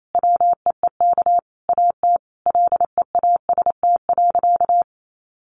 Morse code, heard as a keyed tone, about 28 wpm, 700 Hz. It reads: WEEK AT LEAST.
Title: Sentences from Top 900 Words 28wpm